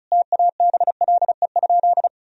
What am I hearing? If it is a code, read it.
Morse code, heard as a keyed tone, about 34 wpm, 700 Hz. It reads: TABLE?
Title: Sentences from Top 1000 Words - Review Part 2 - 35wpm